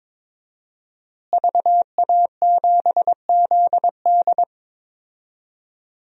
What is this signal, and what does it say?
Morse code, heard as a keyed tone, about 21 wpm, 700 Hz. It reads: VA7ZD
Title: DXCC-100 Call Signs - 5 Characters - Repeated 3x - 22wpm